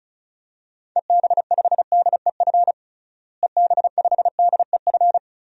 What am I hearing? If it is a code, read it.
Morse code, heard as a keyed tone, about 36 wpm, 700 Hz. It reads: EB5DEF EB5DEF
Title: DXCC-100 Call Signs - 6 Characters - Repeated 3x - 35wpm